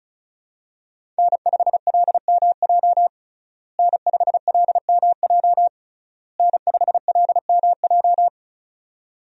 Morse code, heard as a keyed tone, about 36 wpm, 700 Hz. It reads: N5LMJ N5LMJ N5LMJ
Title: USA Call Signs - 5 Characters - Repeated 3x - 35wpm